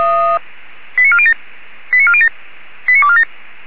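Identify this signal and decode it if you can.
Five-tone selective call (ZVEI), heard as quick runs of five tones, tones of about 70 ms. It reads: 89407 89407 89247
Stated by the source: Received by SDR on a ZVEI selective-call channel.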